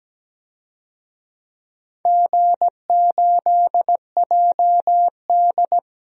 Morse code, heard as a keyed tone, about 17 wpm, 700 Hz. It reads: G8JD
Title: DXCC-100 Call Signs - 4 Characters - Repeated 2x - 17wpm